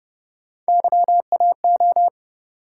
Morse code, heard as a keyed tone, about 32 wpm, 700 Hz. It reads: YAO